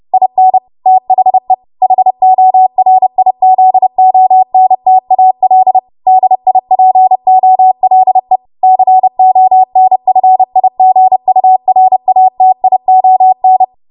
Morse code, heard as a keyed tone, about 30 wpm, 750 Hz. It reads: IN THE HORIZONTAL DIPOLE CONFIGURATION
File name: Morse / 30WPM